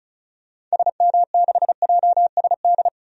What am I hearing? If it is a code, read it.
Morse code, heard as a keyed tone, about 34 wpm, 700 Hz. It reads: SM6JSD
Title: DXCC-100 Call Signs - 6 Characters - 35wpm